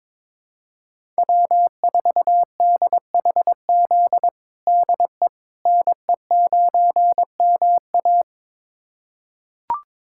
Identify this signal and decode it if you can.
Morse code, heard as a keyed tone, about 21 wpm, 700 Hz. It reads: W4DHZ DE NE9MA
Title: USA Callsign DE Callsign - 5 Characters - 22wpm